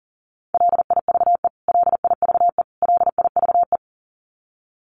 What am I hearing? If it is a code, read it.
Morse code, heard as a keyed tone, about 40 wpm, 700 Hz. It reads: LIVE LIVE LIVE
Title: Top 200 Words Repeated 40wpm